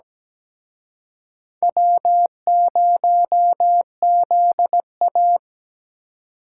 Morse code, heard as a keyed tone, about 20 wpm, 700 Hz. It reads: W0ZA
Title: Speed Racing - USA 1x2 Call Signs 17wpm